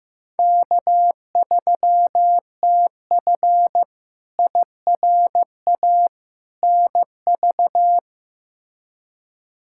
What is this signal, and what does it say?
Morse code, heard as a keyed tone, about 15 wpm, 700 Hz. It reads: K3TF IRA NV
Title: K1USN SST QSO Practice - 15wpm